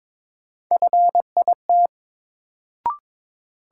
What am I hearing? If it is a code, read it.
Morse code, heard as a keyed tone, about 21 wpm, 700 Hz. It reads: FIT
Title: Sentences from Top 800 Words - New Word Drill - 22wpm